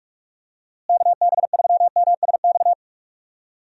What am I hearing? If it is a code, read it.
Morse code, heard as a keyed tone, about 46 wpm, 700 Hz. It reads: KB3MSX
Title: USA Call Signs - 6 Characters - Repeated 3x - 45wpm